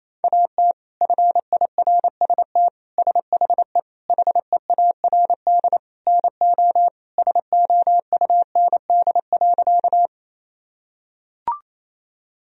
Morse code, heard as a keyed tone, about 28 wpm, 700 Hz. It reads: AT FIRST SHE HEARD NO SOUND.
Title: Sentences from Top 1000 Words - Review Part 1 - 28wpm